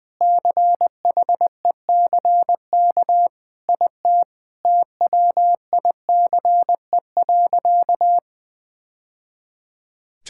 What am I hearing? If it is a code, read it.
Morse code, heard as a keyed tone, about 20 wpm, 700 Hz. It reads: CHECK IT TWICE.
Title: Sentences with 3-Word Phrases - 20wpm